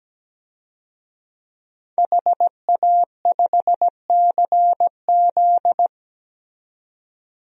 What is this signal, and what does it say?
Morse code, heard as a keyed tone, about 17 wpm, 700 Hz. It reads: HA5CZ
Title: DXCC-100 Call Signs - 5 Characters - 17wpm